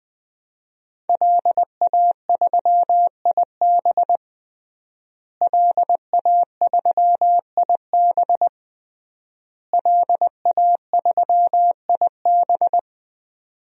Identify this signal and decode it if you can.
Morse code, heard as a keyed tone, about 20 wpm, 700 Hz. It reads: LA3IB LA3IB LA3IB
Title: DXCC-100 Call Signs - 5 Characters - Repeated 3x - 20wpm